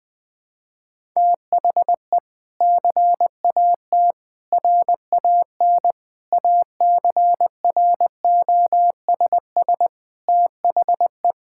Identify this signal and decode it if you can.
Morse code, heard as a keyed tone, about 20 wpm, 700 Hz. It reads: THE CAT RAN ACROSS THE
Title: Sentences with 2-Word Phrases - Spoken Beginning - 20wpm